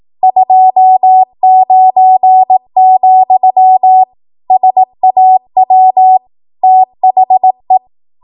Morse code, heard as a keyed tone, about 18 wpm, 750 Hz. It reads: 29, SAW THE
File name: Archive / 18 WPM / 18WPM